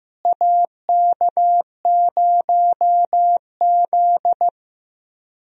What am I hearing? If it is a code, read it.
Morse code, heard as a keyed tone, about 15 wpm, 700 Hz. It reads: AK0Z
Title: USA Call Signs - 4 Characters - Repeated 2x - 15wpm